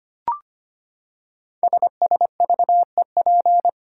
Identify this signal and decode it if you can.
Morse code, heard as a keyed tone, about 25 wpm, 700 Hz. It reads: SSVEP
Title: Code Groups - 5 Letters - 25wpm